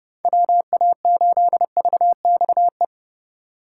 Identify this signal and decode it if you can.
Morse code, heard as a keyed tone, about 34 wpm, 700 Hz. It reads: WA8VXE